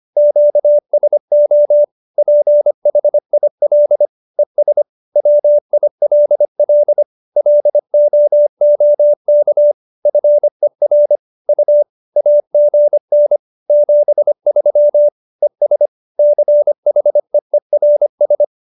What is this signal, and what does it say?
Morse code, heard as a keyed tone, about 25 wpm, 590 Hz. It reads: QSO PHIL ES WILL LOOK FER U AGN 73 ES CHEERS
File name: QSO315 25